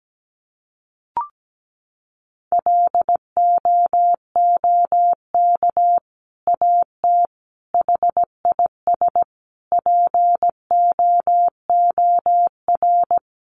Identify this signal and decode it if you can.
Morse code, heard as a keyed tone, about 17 wpm, 700 Hz. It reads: LOOK AT HIS POOR
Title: Sentences from Top 1300 Words 17wpm